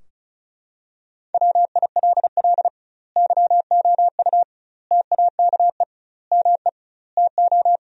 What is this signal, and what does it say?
Morse code, heard as a keyed tone, about 36 wpm, 700 Hz. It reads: WILL YOU TAKE ME TO
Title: 2x Word-Spacing - Sentences from Top 200 Words - 35wpm